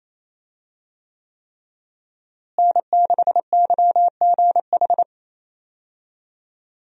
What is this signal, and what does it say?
Morse code, heard as a keyed tone, about 28 wpm, 700 Hz. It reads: N6YGH
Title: USA Call Signs - 5 Characters - Repeated 3x - 28wpm